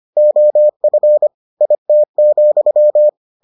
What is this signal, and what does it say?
Morse code, heard as a keyed tone, about 25 wpm, 600 Hz. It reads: OF IT,